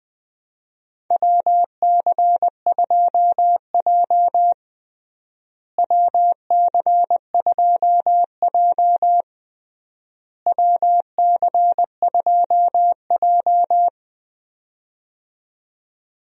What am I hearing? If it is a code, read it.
Morse code, heard as a keyed tone, about 20 wpm, 700 Hz. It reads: WC2J WC2J WC2J
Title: USA Call Signs - 4 Characters - Repeated 3x - 20wpm